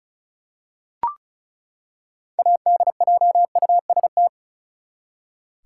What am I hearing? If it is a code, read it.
Morse code, heard as a keyed tone, about 36 wpm, 700 Hz. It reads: ADJUST
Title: Six Letter Words - 35wpm